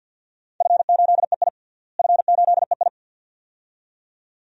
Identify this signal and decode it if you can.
Morse code, heard as a keyed tone, about 48 wpm, 700 Hz. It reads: F8EI F8EI
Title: DXCC-100 Call Signs - 4 Characters - Repeated 2x - 50wpm